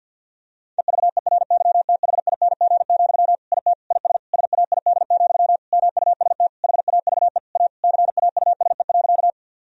Morse code, heard as a keyed tone, about 52 wpm, 700 Hz. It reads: EVERYTHING, IT IS SAID, MUST HAVE A CAUSE.
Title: Sentences from Top 1000 Words - Review Part 1 - 50wpm